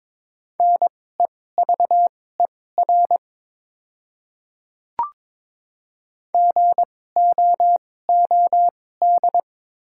Morse code, heard as a keyed tone, about 22 wpm, spaced as for 16 wpm, 700 Hz. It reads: NEVER GOOD
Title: 2x Character Spacing - Top 500 Words - 22wpm